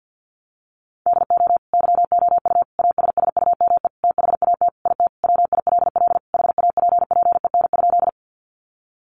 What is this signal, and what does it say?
Morse code, heard as a keyed tone, about 50 wpm, 700 Hz. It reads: DO YOU ASSUME THAT IT WILL HAPPEN?
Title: Sentences from Top 1500 Words 50wpm